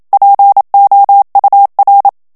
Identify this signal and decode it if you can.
Morse code, heard as a keyed tone, about 27 wpm, 800 Hz. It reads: POUR
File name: FRANCAIS TEXTE CLAIR 2 VITESSE 26 MOTS MINUTES